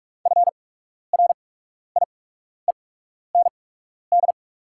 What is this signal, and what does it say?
Morse code, heard as a keyed tone, about 43 wpm, spaced as for 12 wpm, 700 Hz. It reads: FRIEND
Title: 8x Character Spacing - Top 300 Words - 45wpm